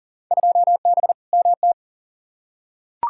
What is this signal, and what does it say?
Morse code, heard as a keyed tone, about 40 wpm, 700 Hz. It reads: 2B MT
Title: Field Day Exchanges Encore - 40wpm